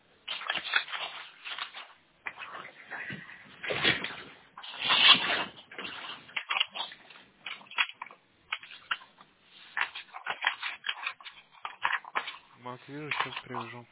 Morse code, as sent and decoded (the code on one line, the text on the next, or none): none